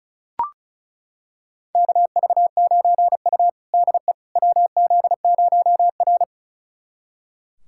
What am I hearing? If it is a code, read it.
Morse code, heard as a keyed tone, about 34 wpm, 700 Hz. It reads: KV9U DE WZ0R
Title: USA Callsign DE Callsign - 4 Characters - 35wpm